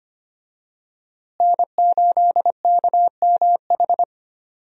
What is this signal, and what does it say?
Morse code, heard as a keyed tone, about 25 wpm, 700 Hz. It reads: N8KMH
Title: USA Call Signs - 5 Characters - Repeated 3x - 25wpm